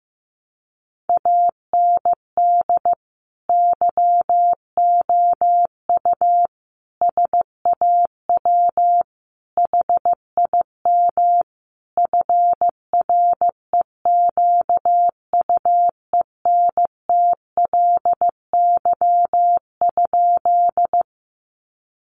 Morse code, heard as a keyed tone, about 15 wpm, 700 Hz. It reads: AND YOU SAW HIM FREQUENTLY?
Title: Sentences from Top 1300 Words 15wpm